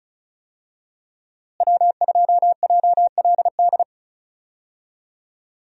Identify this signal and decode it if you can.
Morse code, heard as a keyed tone, about 36 wpm, 700 Hz. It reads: W2JLD
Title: USA Call Signs - 5 Characters - Repeated 2x - 35wpm